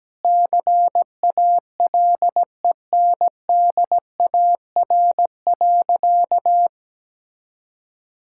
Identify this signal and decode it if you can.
Morse code, heard as a keyed tone, about 17 wpm, 700 Hz. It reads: CALENDAR.
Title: Sentences from Top 1600 Words 17wpm